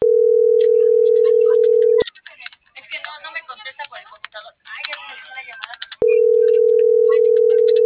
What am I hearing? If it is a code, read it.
Ringback tone (North America), 440 + 480 Hz.